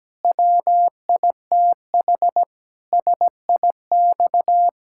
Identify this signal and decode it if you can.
Morse code, heard as a keyed tone, about 17 wpm, 700 Hz. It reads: WITH SIX
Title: Pangram Challenge v2 - 17wpm